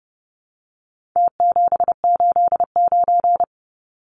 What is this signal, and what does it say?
Morse code, heard as a keyed tone, about 30 wpm, 700 Hz. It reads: T789
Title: Numbers - Contest Style - 4-Digit 30wpm